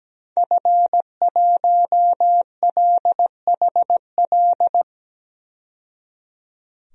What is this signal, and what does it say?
Morse code, heard as a keyed tone, about 17 wpm, 700 Hz. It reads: F1LHL
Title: DXCC-100 Call Signs - 5 Characters - 17wpm